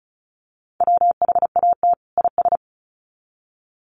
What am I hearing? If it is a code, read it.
Morse code, heard as a keyed tone, about 34 wpm, 700 Hz. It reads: WHAT IS
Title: Sets of 2 Words 35wpm